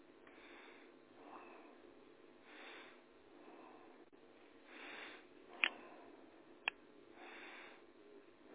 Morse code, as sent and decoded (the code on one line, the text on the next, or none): none